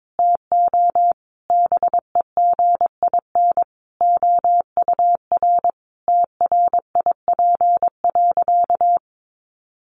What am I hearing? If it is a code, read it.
Morse code, heard as a keyed tone, about 22 wpm, 700 Hz. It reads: TO BEGIN OUR TRIP.
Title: Sentences from Top 1300 Words 22wpm